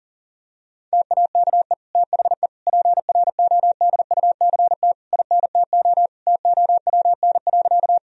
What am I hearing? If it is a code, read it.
Morse code, heard as a keyed tone, about 40 wpm, 700 Hz. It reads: TAKE THE PRODUCT INTO TOWN.
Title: Sentences from Top 1000 Words - Review Part 2 - 40wpm